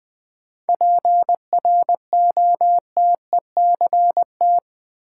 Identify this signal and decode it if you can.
Morse code, heard as a keyed tone, about 21 wpm, 700 Hz. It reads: PROTECT
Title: Sentences from Top 900 Words - New Word Drill - 20wpm